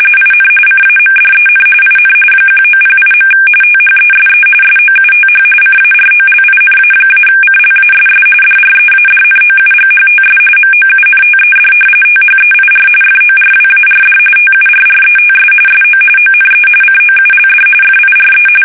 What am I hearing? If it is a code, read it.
RTTY, about 45.45 baud. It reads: IN CAMDEN IN 1908. THESE SOUTH CAROLINA-CLASS DREADNOUGHTS MOUNTED THEIR FORWARD TURRETS SO ONE COULD FIRE
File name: fsk850 / battleshipnj